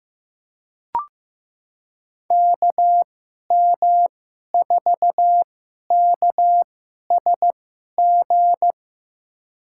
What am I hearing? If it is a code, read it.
Morse code, heard as a keyed tone, about 15 wpm, spaced as for 11 wpm, 700 Hz. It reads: KM4KSG
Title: USA Call Signs - 6 Characters - 2x Character Spacing - 15wpm